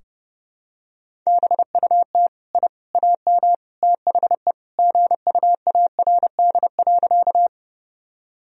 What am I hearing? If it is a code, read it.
Morse code, heard as a keyed tone, about 32 wpm, 700 Hz. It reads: BUT I AM THE GUARD.
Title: Sentences from Top 1000 Words - Review Part 4 - 30wpm